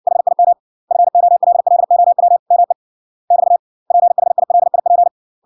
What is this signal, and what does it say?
Morse code, heard as a keyed tone, about 60 wpm, 700 Hz. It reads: HIM FOLLOW ME <BT> PHILIP